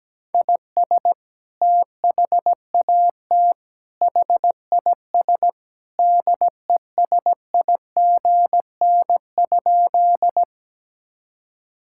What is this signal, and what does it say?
Morse code, heard as a keyed tone, about 17 wpm, 700 Hz. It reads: IS THAT HIS DESIGN?
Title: Sentences from Top 1000 Words - Review Part 5 - 17wpm